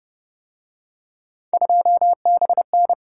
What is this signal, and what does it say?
Morse code, heard as a keyed tone, about 31 wpm, 700 Hz. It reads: 2BN
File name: Speed-Racing - Three Letter-Number Combinations - 25wpm